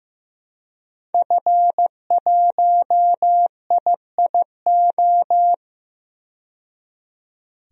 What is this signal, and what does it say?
Morse code, heard as a keyed tone, about 15 wpm, 700 Hz. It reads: F1IIO
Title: DXCC-100 Call Signs - 5 Characters - 15wpm